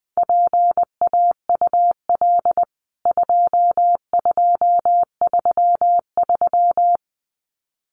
Morse code, heard as a keyed tone, about 19 wpm, 700 Hz. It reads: PAUL 2233
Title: CWT Exchange 20wpm